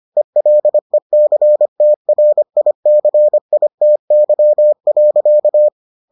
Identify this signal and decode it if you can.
Morse code, heard as a keyed tone, about 25 wpm, 600 Hz. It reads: ELECTRICITY.